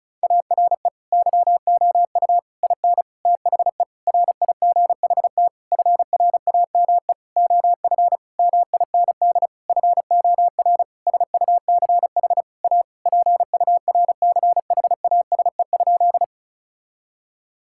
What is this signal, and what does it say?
Morse code, heard as a keyed tone, about 36 wpm, 700 Hz. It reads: ARE YOU IN THE RIGHT FRAME OF MIND FOR SUCH A PURCHASE?
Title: Sentences from Top 1000 Words - Review Part 6 - 35wpm